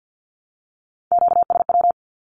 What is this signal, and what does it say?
Morse code, heard as a keyed tone, about 51 wpm, 700 Hz. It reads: QSW
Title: Q-Codes 50wpm